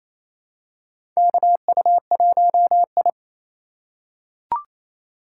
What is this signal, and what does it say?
Morse code, heard as a keyed tone, about 28 wpm, 700 Hz. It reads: KU1I
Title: USA Call Signs - 3x Character Spacing - 28wpm